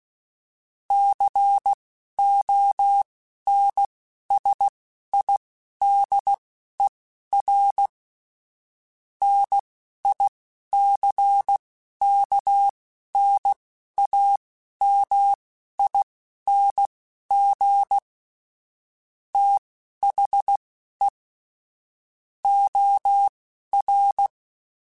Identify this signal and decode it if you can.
Morse code, heard as a keyed tone, about 16 wpm, spaced as for 11 wpm, 800 Hz. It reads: CONSIDER NICKNAMING THE OR
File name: Technical Subjects / Technical Subjects-mp3s / OR Gate @10